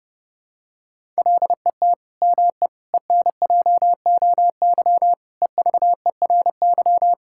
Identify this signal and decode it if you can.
Morse code, heard as a keyed tone, about 30 wpm, 700 Hz. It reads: LET ME ENJOY EVERY
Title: Sentences from Top 1000 Words - Review Part 5 - 30wpm